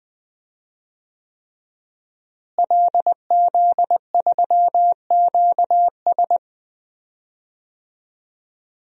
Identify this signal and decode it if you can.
Morse code, heard as a keyed tone, about 21 wpm, 700 Hz. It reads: LZ3QS
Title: DXCC-100 Call Signs - 5 Characters - Repeated 3x - 20wpm